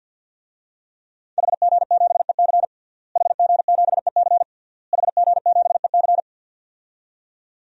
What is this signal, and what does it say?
Morse code, heard as a keyed tone, about 51 wpm, 700 Hz. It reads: HG7EC HG7EC HG7EC
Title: DXCC-100 Call Signs - 5 Characters - Repeated 3x - 50wpm